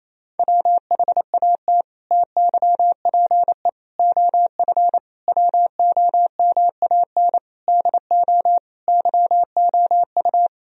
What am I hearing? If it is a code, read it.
Morse code, heard as a keyed tone, about 28 wpm, 700 Hz. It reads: WHAT TYPE OF WOMAN DO YOU
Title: Sentences from Top 1000 Words - Review Part 3 - 28wpm